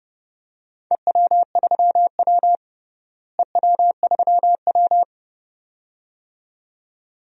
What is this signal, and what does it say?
Morse code, heard as a keyed tone, about 32 wpm, 700 Hz. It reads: EW3W EW3W
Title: DXCC-100 Call Signs - 4 Characters - Repeated 2x - 30wpm